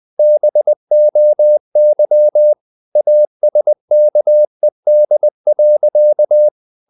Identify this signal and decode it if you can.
Morse code, heard as a keyed tone, about 20 wpm, 600 Hz. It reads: BOY ASKED.
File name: the old man and the sea / the old man and the sea part 1 20wpm